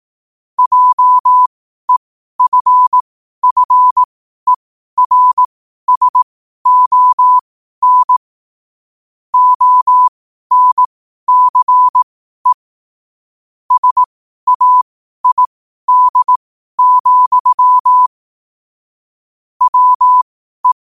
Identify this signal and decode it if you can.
Morse code, heard as a keyed tone, about 18 wpm, spaced as for 12 wpm, 1000 Hz. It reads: JEFFERSON ONCE SAID, WE
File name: qotdfiles / 12wpm